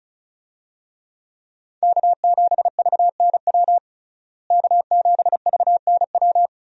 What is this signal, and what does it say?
Morse code, heard as a keyed tone, about 34 wpm, 700 Hz. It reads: K7VNW K7VNW
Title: USA Call Signs - 5 Characters - Repeated 2x - 35wpm